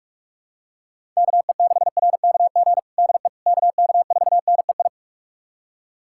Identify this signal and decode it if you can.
Morse code, heard as a keyed tone, about 46 wpm, 700 Hz. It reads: KE6RKG DE KK4NEI
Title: USA Callsign DE Callsign 45wpm